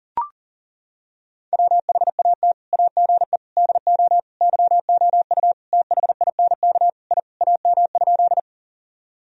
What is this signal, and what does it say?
Morse code, heard as a keyed tone, about 40 wpm, 700 Hz. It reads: WHAT AGE DO YOU THINK I AM?
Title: Sentences from Top 1000 Words - Review Part 3 - 40wpm